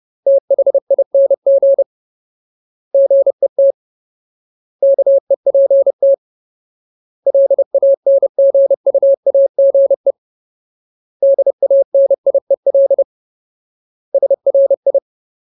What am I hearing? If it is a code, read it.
Morse code, heard as a keyed tone, about 30 wpm, 550 Hz. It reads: THING GET KEPT LANGUAGE DANIEL SRI
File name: Ultimate Word List / 30WPM Ultimate